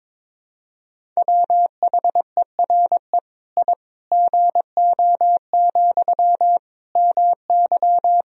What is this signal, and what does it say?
Morse code, heard as a keyed tone, about 22 wpm, 700 Hz. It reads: WHERE I GO, MY
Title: Sentences from Top 1000 Words - Review Part 1 - 22wpm